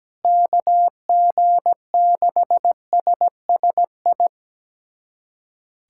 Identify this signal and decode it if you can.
Morse code, heard as a keyed tone, about 17 wpm, 700 Hz. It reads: KG6SSI
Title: USA Callsign DE Callsign - 6 Characters - 17wpm